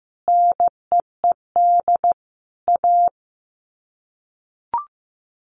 Morse code, heard as a keyed tone, about 15 wpm, 700 Hz. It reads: NEED A
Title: Sets of 5 Words Encore 15wpm